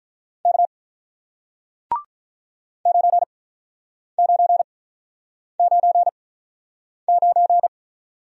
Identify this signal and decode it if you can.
Morse code, heard as a keyed tone, about 47 wpm, spaced as for 9 wpm, 700 Hz. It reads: K 9999